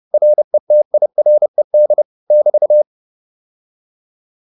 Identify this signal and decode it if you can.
Morse code, heard as a keyed tone, about 30 wpm, 600 Hz. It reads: RETIRED <BT>